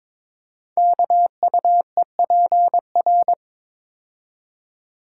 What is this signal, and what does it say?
Morse code, heard as a keyed tone, about 22 wpm, 700 Hz. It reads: KUEPR
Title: Code Groups - 5 Letters - 22wpm